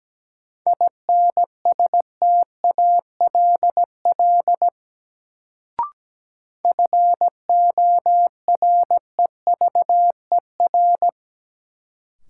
Morse code, heard as a keyed tone, about 17 wpm, 700 Hz. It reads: INSTALL FOREVER
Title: Sentences from Top 1600 Words - New Word Drill - 17wpm